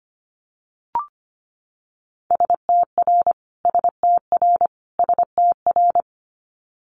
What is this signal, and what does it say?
Morse code, heard as a keyed tone, about 25 wpm, 700 Hz. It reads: STR STR STR
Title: Multi-letter Phonemes Repeated 3x 25wpm